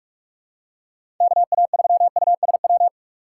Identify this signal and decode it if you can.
Morse code, heard as a keyed tone, about 45 wpm, 700 Hz. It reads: KA3USW
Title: USA Call Signs - 6 Characters - 3x Character Spacing - 45wpm